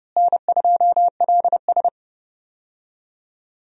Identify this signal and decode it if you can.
Morse code, heard as a keyed tone, about 30 wpm, 700 Hz. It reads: N2LS
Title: USA Call Signs - 4 Characters - Repeated 2x - 30wpm